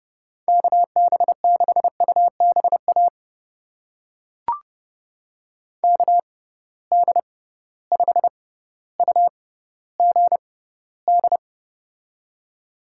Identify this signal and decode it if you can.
Morse code, heard as a keyed tone, about 30 wpm, spaced as for 15 wpm, 700 Hz. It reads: KB6UBA KD5UGD
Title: USA Call Signs - 6x Character Spacing - 30wpm